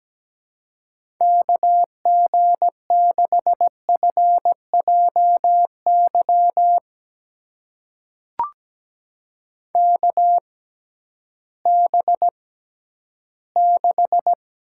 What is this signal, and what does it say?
Morse code, heard as a keyed tone, about 17 wpm, spaced as for 11 wpm, 700 Hz. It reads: KG6FJY KB6
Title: USA Call Signs - 6 Characters - 6x Character Spacing - 17wpm